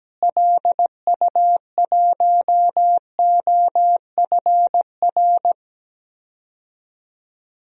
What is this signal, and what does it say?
Morse code, heard as a keyed tone, about 17 wpm, 700 Hz. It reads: LU1OFR
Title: DXCC-100 Call Signs - 6 Characters - Repeated 3x - 17wpm